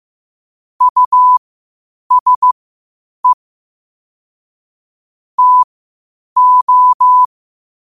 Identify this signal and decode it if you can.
Morse code, heard as a keyed tone, about 15 wpm, spaced as for 8 wpm, 1000 Hz. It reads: USE TO
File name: qotdfiles / 8wpm